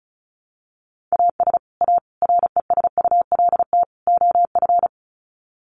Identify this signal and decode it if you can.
Morse code, heard as a keyed tone, about 34 wpm, 700 Hz. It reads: AS A RESULT OF
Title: Sets of 4 Words 35wpm